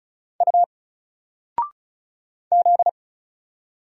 Morse code, heard as a keyed tone, about 35 wpm, 700 Hz. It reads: U Z
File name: Single-Letter-Number-35wpm